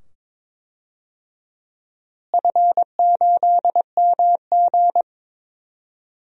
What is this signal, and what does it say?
Morse code, heard as a keyed tone, about 22 wpm, 700 Hz. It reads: F8MG
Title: DXCC-100 Call Signs - 4 Characters - Repeated 2x - 22wpm